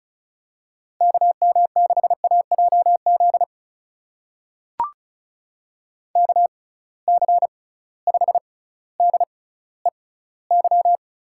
Morse code, heard as a keyed tone, about 36 wpm, spaced as for 18 wpm, 700 Hz. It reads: KM6AJZ KC5DEY
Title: USA Call Signs - 6 Characters - 6x Character Spacing - 35wpm